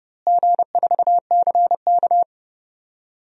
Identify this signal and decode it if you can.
Morse code, heard as a keyed tone, about 31 wpm, 700 Hz. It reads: G4CK
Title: DXCC-100 Call Signs - 4 Characters - Repeated 3x - 30wpm